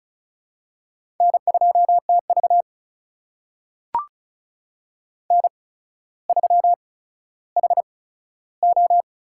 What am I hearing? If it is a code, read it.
Morse code, heard as a keyed tone, about 36 wpm, spaced as for 15 wpm, 700 Hz. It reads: N2TV N3HO
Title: USA Call Signs - 4 Characters - 8x Character Spacing - 35wpm